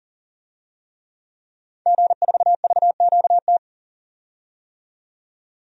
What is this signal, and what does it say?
Morse code, heard as a keyed tone, about 40 wpm, 700 Hz. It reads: G4VQT
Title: DXCC-100 Call Signs - 5 Characters - 40wpm